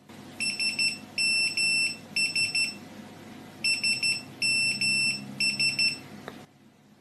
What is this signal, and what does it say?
Morse code, heard as a keyed tone, about 12 wpm, 2700 Hz, with noise. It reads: SMS SMS
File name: nokias-ascending-ringtone-was-morse-code-for-their-slogan-connecting-people